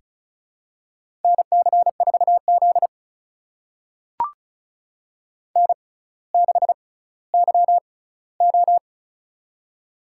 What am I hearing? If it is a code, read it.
Morse code, heard as a keyed tone, about 36 wpm, spaced as for 18 wpm, 700 Hz. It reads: NC4Z N6YO
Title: USA Call Signs - 4 Characters - 6x Character Spacing - 35wpm